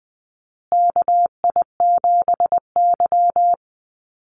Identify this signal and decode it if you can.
Morse code, heard as a keyed tone, about 20 wpm, 700 Hz. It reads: KI7Y
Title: Call Signs Repeated 2x - 20wpm